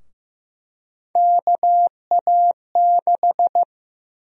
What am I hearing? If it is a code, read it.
Morse code, heard as a keyed tone, about 15 wpm, 700 Hz. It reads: KA6
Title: USA Call Sign Prefixes - 15wpm